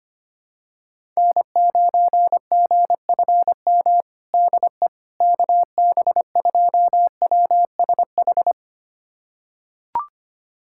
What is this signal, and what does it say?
Morse code, heard as a keyed tone, about 25 wpm, 700 Hz. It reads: N9GFM DE KB2WSH
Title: USA Callsign DE Callsign 25wpm